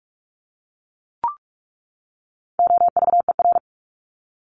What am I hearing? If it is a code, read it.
Morse code, heard as a keyed tone, about 44 wpm, 700 Hz. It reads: OVER
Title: Top 300 Words 45wpm